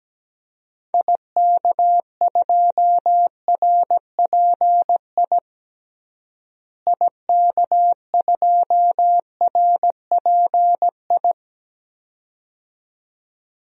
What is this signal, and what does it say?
Morse code, heard as a keyed tone, about 17 wpm, 700 Hz. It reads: IK2RPI IK2RPI
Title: DXCC-100 Call Signs - 6 Characters - Repeated 2x - 17wpm